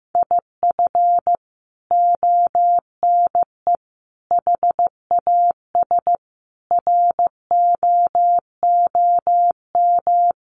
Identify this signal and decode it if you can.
Morse code, heard as a keyed tone, about 15 wpm, 700 Hz. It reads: IF ONE HAS ROOM
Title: Sentences from Top 1400 Words 15wpm